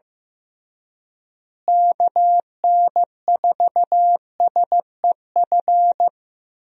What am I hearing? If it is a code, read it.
Morse code, heard as a keyed tone, about 15 wpm, 700 Hz. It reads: KN4SEF
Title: USA Call Signs - 6 Characters - Repeated 2x - 15wpm